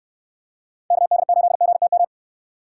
Morse code, heard as a keyed tone, about 69 wpm, 700 Hz. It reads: KD8GTM